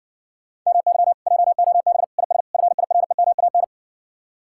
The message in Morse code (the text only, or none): MY JOB IS FIREMAN